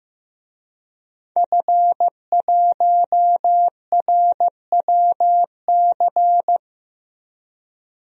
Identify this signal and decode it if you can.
Morse code, heard as a keyed tone, about 15 wpm, 700 Hz. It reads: F1RWC
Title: DXCC-100 Call Signs - 5 Characters - Repeated 3x - 15wpm